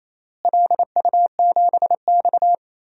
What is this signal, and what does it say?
Morse code, heard as a keyed tone, about 28 wpm, 700 Hz. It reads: LU7X
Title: DXCC-100 Call Signs - 4 Characters - Repeated 3x - 28wpm